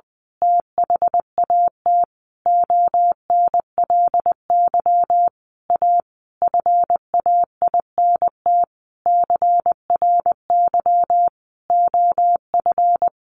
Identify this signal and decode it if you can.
Morse code, heard as a keyed tone, about 20 wpm, 700 Hz. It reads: THAT ONLY A FAINT CRY OF
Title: A Sound Of Thunder 20wpm